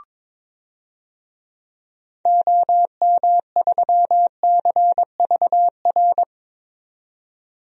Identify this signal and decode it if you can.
Morse code, heard as a keyed tone, about 22 wpm, 700 Hz. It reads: OM3CVR
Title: DXCC-100 Call Signs - 6 Characters - 22wpm